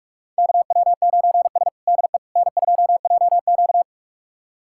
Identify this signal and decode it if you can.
Morse code, heard as a keyed tone, about 46 wpm, 700 Hz. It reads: KW9S DE N2JQ
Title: USA Callsign DE Callsign - 4 Characters - 45wpm